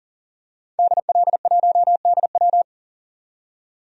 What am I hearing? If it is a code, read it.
Morse code, heard as a keyed tone, about 40 wpm, 700 Hz. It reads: DL1DW
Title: DXCC-100 Call Signs - 5 Characters - 40wpm